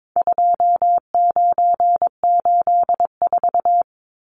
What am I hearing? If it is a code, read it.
Morse code, heard as a keyed tone, about 22 wpm, 700 Hz. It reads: 2984
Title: Numbers - 4-Digit 22wpm